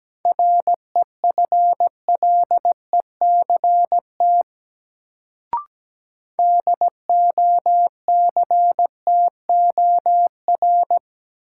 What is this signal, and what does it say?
Morse code, heard as a keyed tone, about 17 wpm, 700 Hz. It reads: REFLECT DOCTOR
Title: Sentences from Top 1500 Words - New Word Drill Alt - 17wpm